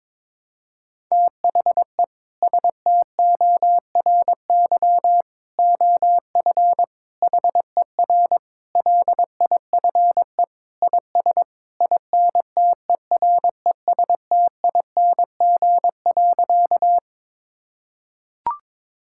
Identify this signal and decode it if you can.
Morse code, heard as a keyed tone, about 22 wpm, 700 Hz. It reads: THE STORY OF HER LIFE IS INTERESTING.
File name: practice / 22wpm / Sentences with 3-Word Phrases - 22wpm